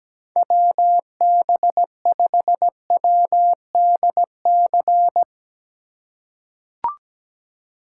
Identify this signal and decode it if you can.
Morse code, heard as a keyed tone, about 17 wpm, 700 Hz. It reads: WB5WDC
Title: USA Call Signs - 6 Characters - 3x Character Spacing - 17wpm